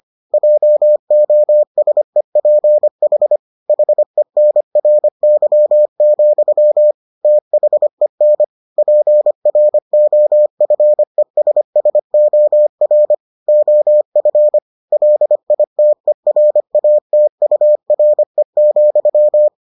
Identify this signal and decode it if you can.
Morse code, heard as a keyed tone, about 25 wpm, 600 Hz. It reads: JOSEPH HENRY, THEN PROFESSOR OF LITERATURE,